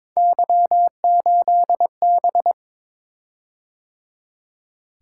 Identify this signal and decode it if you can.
Morse code, heard as a keyed tone, about 23 wpm, 700 Hz. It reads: Y8B
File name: practice / 22wpm / Three Letter-Number Combinations - 22wpm